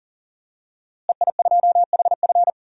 Morse code, heard as a keyed tone, about 40 wpm, 700 Hz. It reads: EI2HF
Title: DXCC-100 Call Signs - 5 Characters - Repeated 2x - 40wpm